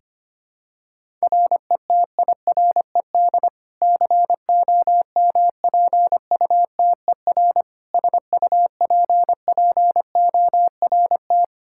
Morse code, heard as a keyed tone, about 25 wpm, 700 Hz. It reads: RETIRED COMPUTER SUPPORT
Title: Practice QSO - Exchange 3 - More Info - 25wpm